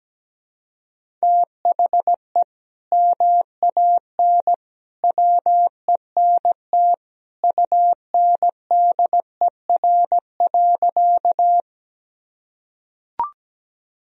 Morse code, heard as a keyed tone, about 17 wpm, 700 Hz. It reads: THE MAN WENT UNDER.
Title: Sentences from Top 1000 Words - Review Part 1 - 17wpm